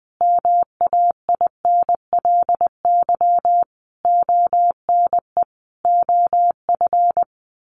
Morse code, heard as a keyed tone, about 20 wpm, 700 Hz. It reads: MAINLY ONE OF
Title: Sentences from Top 1400 Words 20wpm